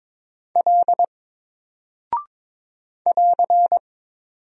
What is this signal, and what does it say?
Morse code, heard as a keyed tone, about 22 wpm, 700 Hz. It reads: L <AR>
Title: Single Character (Letter:Number:Punctuation:Prosign) 22wpm